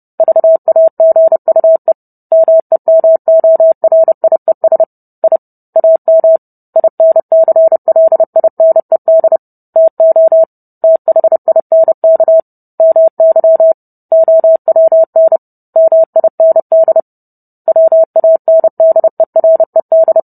Morse code, heard as a keyed tone, about 30 wpm, 650 Hz. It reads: VAGUE MEMORIES I AM INCLINED TO THINK MY OWN MIND WANDERED
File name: Book Two / 04-Chapter 4 - 30 WPM